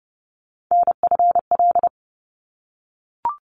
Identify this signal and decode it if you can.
Morse code, heard as a keyed tone, about 30 wpm, 700 Hz. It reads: NFL
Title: ARRL Sections 30wpm